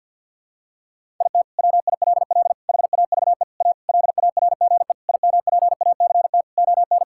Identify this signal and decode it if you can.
Morse code, heard as a keyed tone, about 52 wpm, 700 Hz. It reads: IT WILL HAVE A LARGE IMPACT ON
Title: Sentences from Top 1000 Words - Review Part 6 - 50wpm